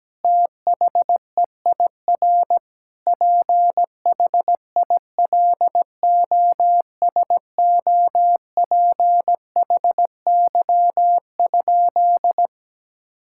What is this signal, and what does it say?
Morse code, heard as a keyed tone, about 17 wpm, 700 Hz. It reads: THEIR PHILOSOPHY?
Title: Sentences from Top 1000 Words - Review Part 6 - 17wpm